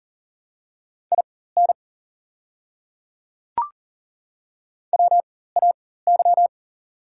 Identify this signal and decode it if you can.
Morse code, heard as a keyed tone, about 40 wpm, spaced as for 19 wpm, 700 Hz. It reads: IN WAY
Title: 4x Character Spacing - Top 100 Words - 40wpm